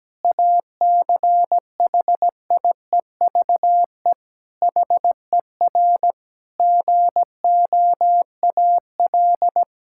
Morse code, heard as a keyed tone, about 17 wpm, 700 Hz. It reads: ACHIEVE HER GOAL
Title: Sentences from Top 1000 Words - Review Part 6 - 17wpm